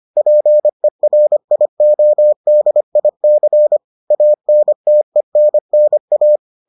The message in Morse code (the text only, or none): PERIODIC ANTENNA